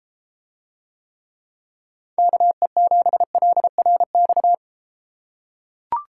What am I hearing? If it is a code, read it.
Morse code, heard as a keyed tone, about 32 wpm, 700 Hz. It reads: KE7LRX